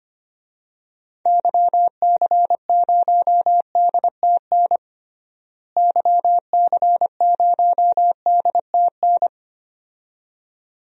Morse code, heard as a keyed tone, about 25 wpm, 700 Hz. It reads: YC0DTN YC0DTN
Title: DXCC-100 Call Signs - 6 Characters - Repeated 2x - 25wpm